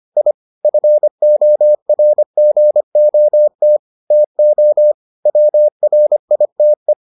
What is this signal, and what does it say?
Morse code, heard as a keyed tone, about 25 wpm, 600 Hz. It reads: I FORGOT TO WRITE